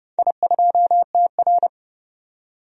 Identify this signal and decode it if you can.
Morse code, heard as a keyed tone, about 29 wpm, 700 Hz. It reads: I2TR
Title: DXCC-100 Call Signs - 4 Characters - 30wpm